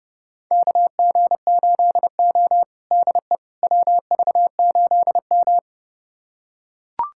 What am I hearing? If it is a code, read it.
Morse code, heard as a keyed tone, about 30 wpm, 700 Hz. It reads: KG8O DE WV8M
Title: USA Callsign DE Callsign - 4 Characters - 30wpm